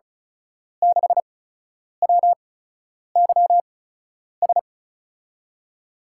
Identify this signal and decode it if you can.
Morse code, heard as a keyed tone, about 36 wpm, spaced as for 10 wpm, 700 Hz. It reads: 6WYS
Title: USA Call Signs - 5 Characters - 8x Character Spacing - 35wpm